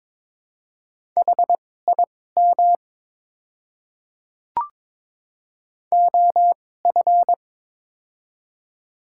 Morse code, heard as a keyed tone, about 22 wpm, spaced as for 16 wpm, 700 Hz. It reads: HIM OF
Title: 2x Character Spacing - Top 100 Words - 22wpm